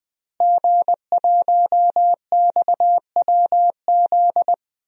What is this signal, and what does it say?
Morse code, heard as a keyed tone, about 21 wpm, 700 Hz. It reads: G1XWZ
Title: DXCC-100 Call Signs - 5 Characters - Repeated 3x - 20wpm